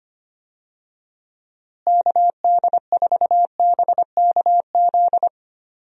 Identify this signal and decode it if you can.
Morse code, heard as a keyed tone, about 25 wpm, 700 Hz. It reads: KD4BKZ
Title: USA Call Signs - 6 Characters - Repeated 3x - 25wpm